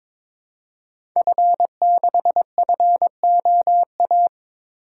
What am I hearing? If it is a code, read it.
Morse code, heard as a keyed tone, about 21 wpm, 700 Hz. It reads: F6FOA
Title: Code Groups - 5 Letter:Numbers - 22wpm